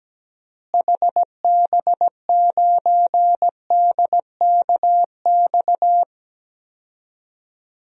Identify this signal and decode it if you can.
Morse code, heard as a keyed tone, about 17 wpm, 700 Hz. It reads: HB9DKX
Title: DXCC-100 Call Signs - 6 Characters - Repeated 3x - 17wpm